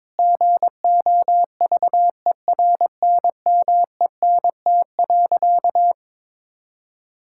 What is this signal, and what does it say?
Morse code, heard as a keyed tone, about 22 wpm, 700 Hz. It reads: GOVERNMENT.
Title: Sentences from Top 1000 Words - Review Part 5 - 22wpm